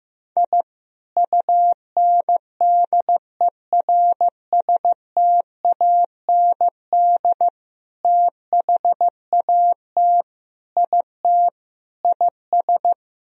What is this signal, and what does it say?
Morse code, heard as a keyed tone, about 15 wpm, 700 Hz. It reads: I UNDERSTAND THAT IT IS
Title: Sentences from Top 1000 Words - Review Part 1 - 15wpm